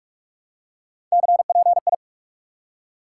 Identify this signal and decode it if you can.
Morse code, heard as a keyed tone, about 46 wpm, 700 Hz. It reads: CPI
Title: Top 90 QSO Elements 45wpm